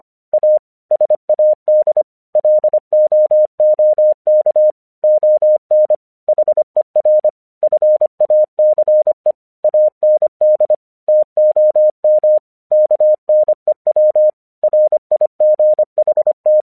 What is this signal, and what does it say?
Morse code, heard as a keyed tone, about 25 wpm, 610 Hz. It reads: A SAD LOOK ON HER FACE AND TOM KNEW RIGHT
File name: 1 English 25wpm / AQC02